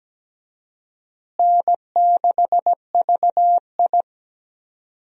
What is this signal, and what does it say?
Morse code, heard as a keyed tone, about 17 wpm, 700 Hz. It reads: N6VI
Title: USA Call Signs - 4 Characters - Repeated 2x - 17wpm